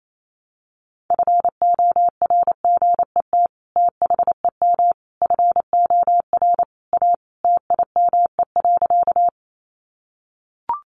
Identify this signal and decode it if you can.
Morse code, heard as a keyed tone, about 28 wpm, 700 Hz. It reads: FORGET THEM FOR A TIME.
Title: Sentences from Top 1200 Words 28wpm